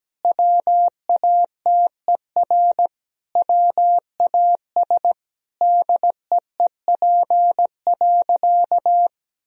Morse code, heard as a keyed tone, about 17 wpm, 700 Hz. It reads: WATER WAS DEEP.
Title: Sentences from Top 1000 Words - Review Part 3 - 17wpm